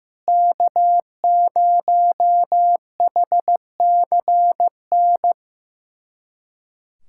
Morse code, heard as a keyed tone, about 15 wpm, 700 Hz. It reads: K0HCN
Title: USA Callsign DE Callsign - 5 Characters - 15wpm